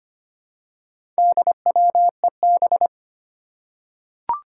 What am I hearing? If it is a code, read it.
Morse code, heard as a keyed tone, about 25 wpm, 700 Hz. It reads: DWEB
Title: Four Challenging Letter-Number Combinations 25wpm